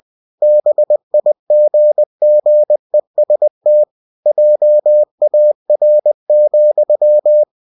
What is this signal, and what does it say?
Morse code, heard as a keyed tone, about 20 wpm, 600 Hz. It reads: BIGGEST JAR,
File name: cw-books / winnie-the-pooh / winnie-20wpm-page-0095